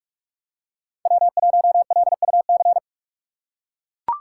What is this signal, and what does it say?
Morse code, heard as a keyed tone, about 43 wpm, 700 Hz. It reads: W1LUC
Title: USA Call Signs - 5 Characters - 2x Character Spacing - 45wpm